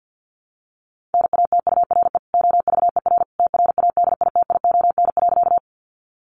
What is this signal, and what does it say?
Morse code, heard as a keyed tone, about 50 wpm, 700 Hz. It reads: NATURE OVER TRADITION.
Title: Sentences from Top 1500 Words 50wpm